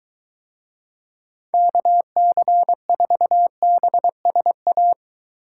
Morse code, heard as a keyed tone, about 21 wpm, 700 Hz. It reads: KC4BSA